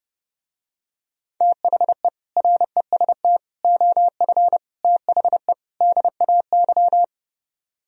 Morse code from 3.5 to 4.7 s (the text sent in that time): OF